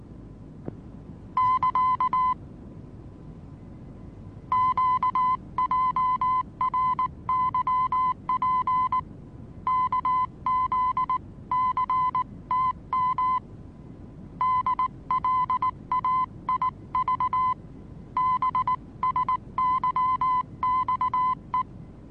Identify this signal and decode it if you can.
Morse code, heard as a keyed tone, about 19 wpm, 1020 Hz. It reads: <KA> QJRYP KZCTM DLAIV BSYXE
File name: repos / Morsekurs / DJ4UF Morsen26